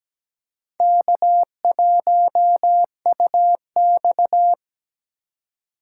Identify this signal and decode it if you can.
Morse code, heard as a keyed tone, about 20 wpm, 700 Hz. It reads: K1UX
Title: Speed Racing - USA 1x2 Call Signs 17wpm